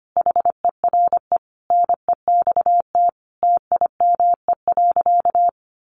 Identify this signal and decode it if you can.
Morse code, heard as a keyed tone, about 25 wpm, 700 Hz. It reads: HERE NEXT TIME.
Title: Sentences from Top 1500 Words 25wpm